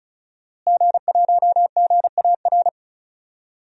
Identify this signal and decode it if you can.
Morse code, heard as a keyed tone, about 36 wpm, 700 Hz. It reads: G1GAR
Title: DXCC-100 Call Signs - 5 Characters - Repeated 2x - 35wpm